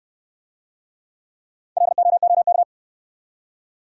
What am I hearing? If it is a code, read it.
Morse code, heard as a keyed tone, about 66 wpm, 700 Hz. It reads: LOOK